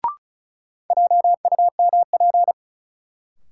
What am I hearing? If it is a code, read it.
Morse code, heard as a keyed tone, about 36 wpm, 700 Hz. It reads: JUMP
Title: Sentences from Top 1300 Words - New Word Drill Alt - 35wpm